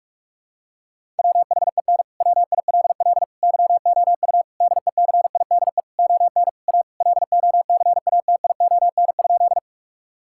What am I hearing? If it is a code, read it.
Morse code, heard as a keyed tone, about 46 wpm, 700 Hz. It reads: WHEN WILL YOU DECIDE ON A LOCATION?
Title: Sentences from Top 1000 Words - Review Part 6 - 45wpm